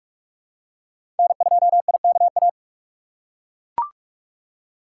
Morse code, heard as a keyed tone, about 41 wpm, 700 Hz. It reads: N2IKA
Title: USA Call Signs - 5 Characters - 2x Character Spacing - 45wpm